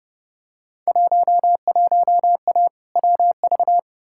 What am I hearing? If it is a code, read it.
Morse code, heard as a keyed tone, about 30 wpm, 700 Hz. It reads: 11A WV
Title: Field Day Exchanges Encore - 30wpm